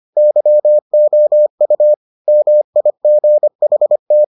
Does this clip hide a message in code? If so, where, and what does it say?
Yes: Morse code from 0.2 to 4.2 s, heard as a keyed tone, about 25 wpm, 600 Hz. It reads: YOU MIGHT